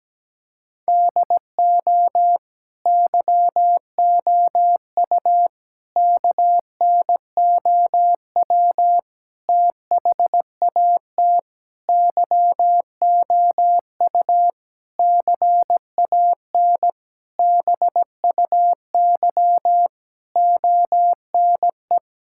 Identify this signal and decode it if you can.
Morse code, heard as a keyed tone, about 17 wpm, 700 Hz. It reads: DO YOU KNOW THAT YOU CAN BUY ONE
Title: Sentences from Top 1000 Words - Review Part 5 - 17wpm